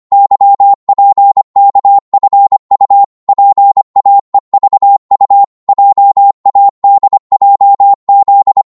Morse code, heard as a keyed tone, about 25 wpm, 800 Hz. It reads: YPKFU PAEVU JADJZ